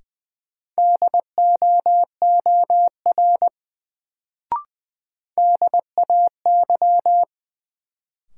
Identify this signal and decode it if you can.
Morse code, heard as a keyed tone, about 20 wpm, 700 Hz. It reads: DOOR DAY
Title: Sets of 2 Words - New Word Drill - 20wpm